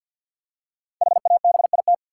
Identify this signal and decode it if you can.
Morse code, heard as a keyed tone, about 50 wpm, 700 Hz. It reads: HABIT
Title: Five Letter Words - 50wpm